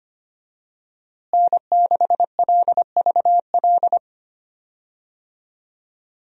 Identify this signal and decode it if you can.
Morse code, heard as a keyed tone, about 25 wpm, 700 Hz. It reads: N6LVL
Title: USA Call Signs - 5 Characters - Repeated 2x - 25wpm